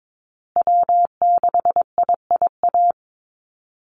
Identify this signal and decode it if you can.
Morse code, heard as a keyed tone, about 22 wpm, 700 Hz. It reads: W6IIA
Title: USA Call Signs Repeated 2x 22wpm